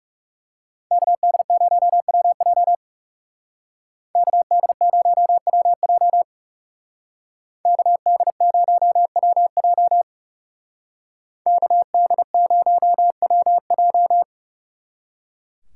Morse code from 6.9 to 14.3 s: KD0WJ KD0WJ

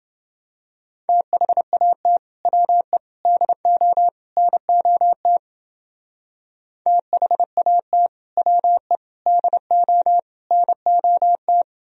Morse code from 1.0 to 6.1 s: THAT WE DO NOT

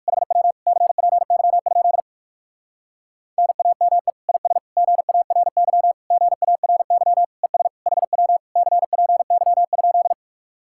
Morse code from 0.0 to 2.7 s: HW CPY?

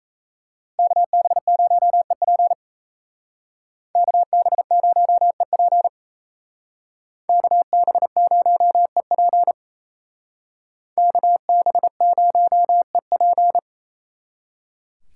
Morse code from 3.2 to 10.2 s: KB0EP KB0EP

